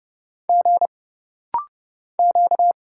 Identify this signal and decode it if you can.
Morse code, heard as a keyed tone, about 30 wpm, 700 Hz. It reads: G Q
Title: Single Letters - Rapid-Fire 30wpm